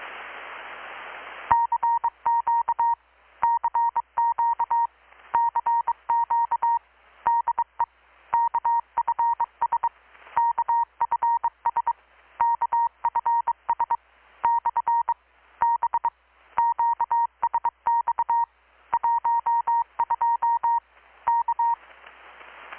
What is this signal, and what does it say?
Morse code, heard as a keyed tone, about 24 wpm, 940 Hz. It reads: CQ CQ CQ DE KFS KFS KFS / B QSX 12 K
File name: blog / swl / 6-abril-2019-12695-USB-KPS